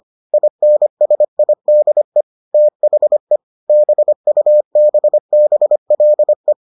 Morse code, heard as a keyed tone, about 25 wpm, 600 Hz. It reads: INSIDE THE BUBBLE